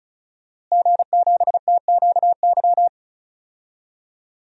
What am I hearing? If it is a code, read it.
Morse code, heard as a keyed tone, about 34 wpm, 700 Hz. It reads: G7TQY
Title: DXCC-100 Call Signs - 5 Characters - Repeated 3x - 35wpm